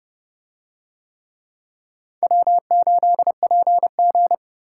Morse code, heard as a keyed tone, about 29 wpm, 700 Hz. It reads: W8PG